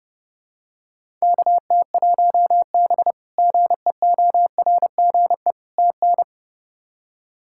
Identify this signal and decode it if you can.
Morse code, heard as a keyed tone, about 29 wpm, 700 Hz. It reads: KT1B GEORGE TN
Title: K1USN SST QSO Practice - 30wpm